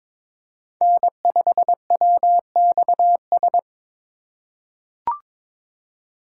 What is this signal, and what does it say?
Morse code, heard as a keyed tone, about 21 wpm, 700 Hz. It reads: N5WXS
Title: USA Call Signs - 5 Characters - 3x Character Spacing - 22wpm